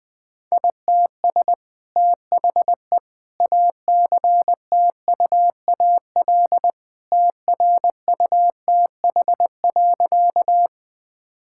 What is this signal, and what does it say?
Morse code, heard as a keyed tone, about 20 wpm, 700 Hz. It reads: ITS THE ACTUAL TRUTH.
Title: Sentences from Top 1000 Words - Review Part 5 - 20wpm